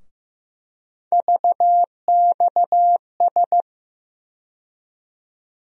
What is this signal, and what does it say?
Morse code, heard as a keyed tone, about 15 wpm, 700 Hz. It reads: VXS
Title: USA Call Sign Postfixes - 3 Letters 15wpm